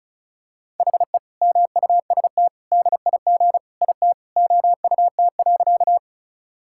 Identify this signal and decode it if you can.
Morse code, heard as a keyed tone, about 36 wpm, 700 Hz. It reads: HE MUST DIG IT OUT.
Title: Sentences from Top 1600 Words 35wpm